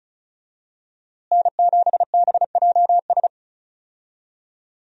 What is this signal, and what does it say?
Morse code, heard as a keyed tone, about 34 wpm, 700 Hz. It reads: N7BJS
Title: USA Call Signs - 5 Characters - Repeated 2x - 35wpm